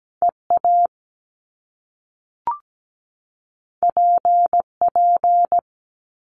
Morse code, heard as a keyed tone, about 17 wpm, 700 Hz. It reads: EA PP
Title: Top Two Letter Combinations in Words 17wpm